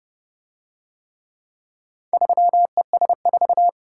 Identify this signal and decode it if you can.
Morse code, heard as a keyed tone, about 27 wpm, 700 Hz. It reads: 3ES4